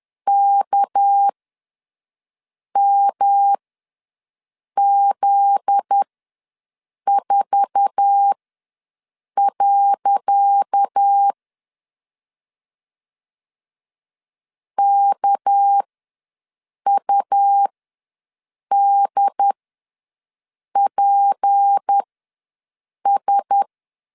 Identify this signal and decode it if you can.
Morse code, heard as a keyed tone, about 10 wpm, spaced as for 6 wpm, 800 Hz. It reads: KMZ4. KUDPS
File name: LEZIONI / 10 SIGNS / 10C DOT ws 10wpm 4213